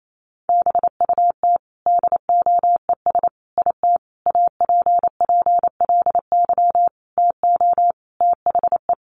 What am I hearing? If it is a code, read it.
Morse code, heard as a keyed tone, about 28 wpm, 700 Hz. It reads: BUT DOES IT APPLY TO THE
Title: Sentences from Top 1200 Words 28wpm